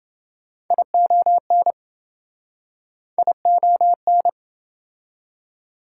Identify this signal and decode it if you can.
Morse code, heard as a keyed tone, about 27 wpm, 700 Hz. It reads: ION ION